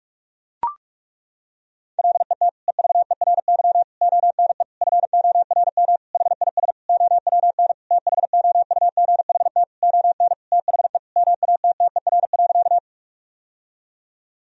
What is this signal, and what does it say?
Morse code, heard as a keyed tone, about 41 wpm, 700 Hz. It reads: LET EVERY ONE FORM HIS OWN THOUGHT ON THE MATTER.
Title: Sentences from Top 1000 Words - Review Part 1 - 45wpm